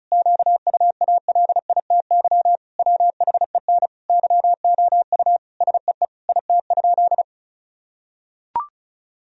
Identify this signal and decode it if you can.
Morse code, heard as a keyed tone, about 36 wpm, 700 Hz. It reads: QUALITY WHEN YOU SEE IT?
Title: Sentences from Top 1000 Words - Review Part 4 - 35wpm